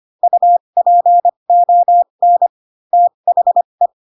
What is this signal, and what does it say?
Morse code, heard as a keyed tone, about 25 wpm, spaced as for 22 wpm, 700 Hz. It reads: UPON THE
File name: Pinocchio / Pinocchio ch2 18 25 ik1hge